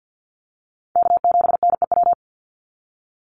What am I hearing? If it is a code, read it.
Morse code, heard as a keyed tone, about 48 wpm, 700 Hz. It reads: K7NEW
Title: NA SOTA Chaser Callsigns 50wpm